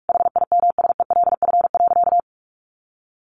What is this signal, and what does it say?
Morse code, heard as a keyed tone, about 44 wpm, 700 Hz. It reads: HIMSELF.
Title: A Sound Of Thunder 45wpm